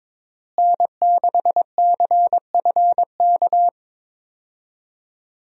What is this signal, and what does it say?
Morse code, heard as a keyed tone, about 22 wpm, 700 Hz. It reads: N6CFK
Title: USA Call Signs - 5 Characters - Repeated 3x - 22wpm